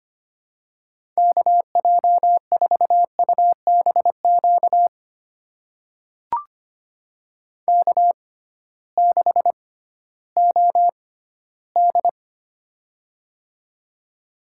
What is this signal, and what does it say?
Morse code, heard as a keyed tone, about 25 wpm, spaced as for 14 wpm, 700 Hz. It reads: KJ4UBQ K6OD
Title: USA Call Signs - 6x Character Spacing - 25wpm